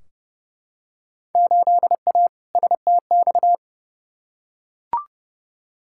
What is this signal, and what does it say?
Morse code, heard as a keyed tone, about 30 wpm, 700 Hz. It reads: 8A STX
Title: Field Day Exchanges Encore - 30wpm